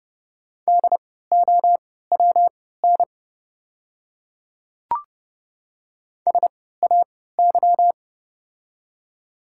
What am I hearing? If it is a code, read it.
Morse code, heard as a keyed tone, about 30 wpm, spaced as for 17 wpm, 700 Hz. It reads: DOWN SAY
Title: 3x Character Spacing - Top 300 Words - 30wpm